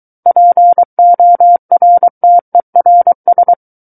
Morse code, heard as a keyed tone, about 23 wpm, 700 Hz. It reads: PORTERS
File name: Book One / 14-Chapter 14 - 23 WPM